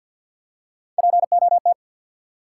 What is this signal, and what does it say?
Morse code, heard as a keyed tone, about 53 wpm, 700 Hz. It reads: POT
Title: Three Letter Words - 50wpm